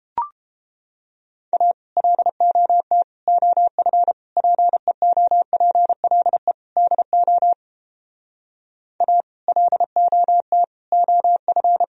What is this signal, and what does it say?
Morse code, heard as a keyed tone, about 29 wpm, 700 Hz. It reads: A LOT OF PEOPLE DO A LOT OF